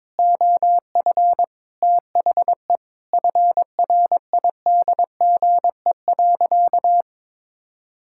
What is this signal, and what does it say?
Morse code, heard as a keyed tone, about 22 wpm, 700 Hz. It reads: OF THE FRIDGE.
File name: practice / 22wpm / Sentences with 4-Word Phrases - 22wpm